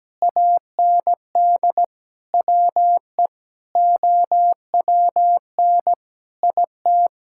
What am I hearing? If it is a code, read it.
Morse code, heard as a keyed tone, about 17 wpm, 700 Hz. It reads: AND WE OWN IT
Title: Sentences from Top 1600 Words 17wpm